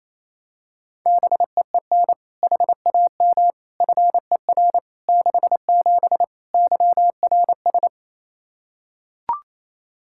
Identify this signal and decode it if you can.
Morse code, heard as a keyed tone, about 28 wpm, 700 Hz. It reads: BEEN HAM FER 67 YRS
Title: Practice QSO - Exchange 3 - More Info - 28wpm